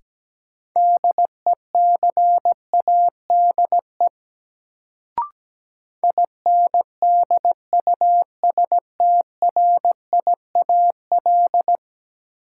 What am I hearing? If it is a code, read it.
Morse code, heard as a keyed tone, about 17 wpm, 700 Hz. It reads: DECADE INDUSTRIAL
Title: Sets of 5 Words - New Word Drill - 17wpm